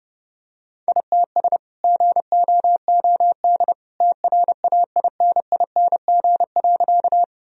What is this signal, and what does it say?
Morse code, heard as a keyed tone, about 30 wpm, 700 Hz. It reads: ITS GOOD TRAINING.
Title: Sentences from Top 1000 Words - Review Part 4 - 30wpm